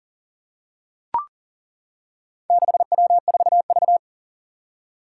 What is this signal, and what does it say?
Morse code, heard as a keyed tone, about 42 wpm, 700 Hz. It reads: 6W4V
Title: Four Challenging Letter-Number Combinations 40wpm